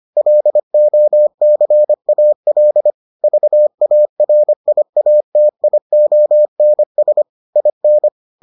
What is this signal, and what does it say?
Morse code, heard as a keyed tone, about 25 wpm, 600 Hz. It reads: LOCAL VARIATIONS IN